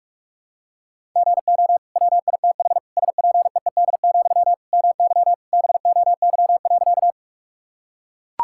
Morse code, heard as a keyed tone, about 44 wpm, 700 Hz. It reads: GO WITH SPEED, MY BOY.
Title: Sentences from Top 1000 Words - Review Part 5 - 45wpm